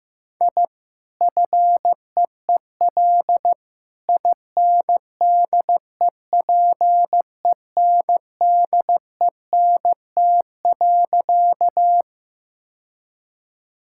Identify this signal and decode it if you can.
Morse code, heard as a keyed tone, about 15 wpm, 700 Hz. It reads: I FEEL INDEPENDENT.
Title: Sentences from Top 1000 Words - Review Part 6 - 15wpm